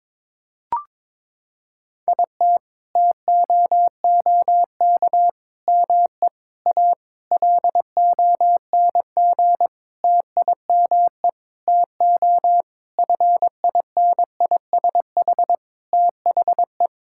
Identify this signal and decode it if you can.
Morse code, heard as a keyed tone, about 22 wpm, 700 Hz. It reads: IT TOOK ME A LONG TIME TO FINISH THE
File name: Sentences with 7-Word Phrases - 22wpm